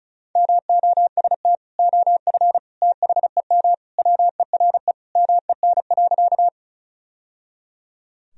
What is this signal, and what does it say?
Morse code, heard as a keyed tone, about 36 wpm, 700 Hz. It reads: MOST OF THEM WERE MEN.
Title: Sentences from Top 1000 Words - Review Part 1 - 35wpm